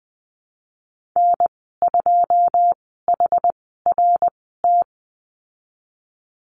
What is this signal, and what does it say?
Morse code, heard as a keyed tone, about 20 wpm, spaced as for 15 wpm, 700 Hz. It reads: N2HRT
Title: 088 Lesson 15g - Number 5 - Call Signs 20wpm